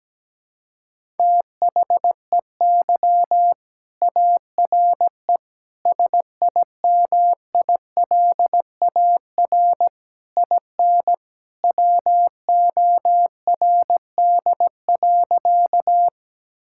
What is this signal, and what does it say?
Morse code, heard as a keyed tone, about 17 wpm, 700 Hz. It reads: THEY ARE SIMILAR IN WORD.
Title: Sentences from Top 1000 Words - Review Part 5 - 17wpm